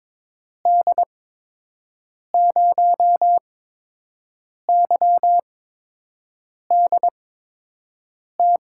Morse code, heard as a keyed tone, about 23 wpm, spaced as for 6 wpm, 700 Hz. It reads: D0YDT
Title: USA Call Signs - 6 Characters - 8x Character Spacing - 22wpm